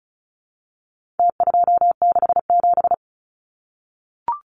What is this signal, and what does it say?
Morse code, heard as a keyed tone, about 34 wpm, 700 Hz. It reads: T267
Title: Numbers - Contest Style - 4-Digit 35wpm